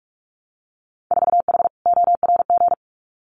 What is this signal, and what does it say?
Morse code, heard as a keyed tone, about 46 wpm, 700 Hz. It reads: 4H ORG
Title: Winter Field Day Exchanges Encore 45wpm